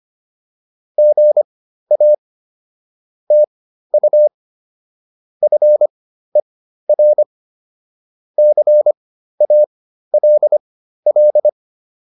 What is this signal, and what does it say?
Morse code, heard as a keyed tone, about 25 wpm, spaced as for 13 wpm, 600 Hz. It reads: GA TU FER CALL
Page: intermediate-practice-files